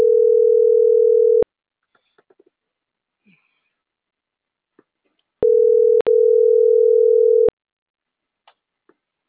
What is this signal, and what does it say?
Ringback tone (North America), 440 + 480 Hz.
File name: CENTRAL MAERSK / out luis.herrera - .85012